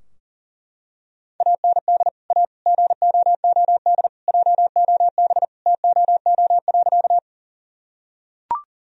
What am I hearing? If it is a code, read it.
Morse code, heard as a keyed tone, about 40 wpm, 700 Hz. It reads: AND A GOOD JOB TOO.
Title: Sentences from Top 1000 Words - Review Part 3 - 40wpm